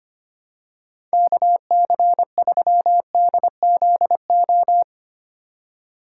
Morse code, heard as a keyed tone, about 25 wpm, 700 Hz. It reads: KC3DZO
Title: USA Call Signs - 6 Characters - 2x Character Spacing - 25wpm